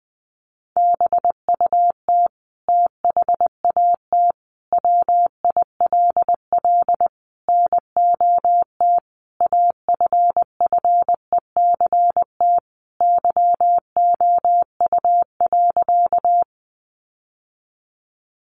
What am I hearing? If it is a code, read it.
Morse code, heard as a keyed tone, about 20 wpm, 700 Hz. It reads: BUT THAT WILL NOT AFFECT YOU.
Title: Sentences from Top 900 Words 20wpm